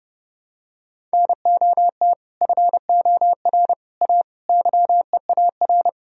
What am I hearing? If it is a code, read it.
Morse code, heard as a keyed tone, about 30 wpm, 700 Hz. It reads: NOT FOR A YEAR
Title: Sentences from Top 1000 Words - Review Part 1 - 30wpm